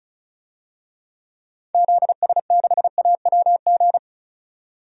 Morse code, heard as a keyed tone, about 36 wpm, 700 Hz. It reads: ZS6AWG
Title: DXCC-100 Call Signs - 6 Characters - 35wpm